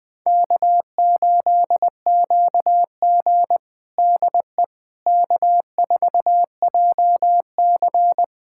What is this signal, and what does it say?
Morse code, heard as a keyed tone, about 21 wpm, 700 Hz. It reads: K8QG DE K4JC
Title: USA Callsign DE Callsign - 4 Characters - 20wpm